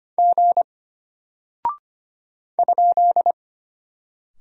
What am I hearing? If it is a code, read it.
Morse code, heard as a keyed tone, about 25 wpm, 700 Hz. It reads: G ?
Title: Single Character (Letter:Number:Punctuation:Prosign) Alt 25wpm